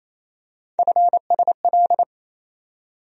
Morse code, heard as a keyed tone, about 28 wpm, 700 Hz. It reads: FSL
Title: USA Call Sign Postfixes - 3 Letters 28wpm